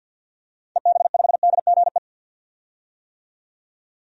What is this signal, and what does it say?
Morse code, heard as a keyed tone, about 51 wpm, 700 Hz. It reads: EB5DGE
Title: DXCC-100 Call Signs - 6 Characters - 50wpm